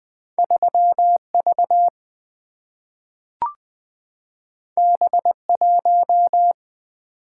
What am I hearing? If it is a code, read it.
Morse code, heard as a keyed tone, about 20 wpm, 700 Hz. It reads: 3 V B 1
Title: Single-Letter Single-Number 20wpm